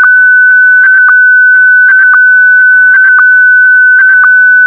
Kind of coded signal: SSTV (slow-scan television)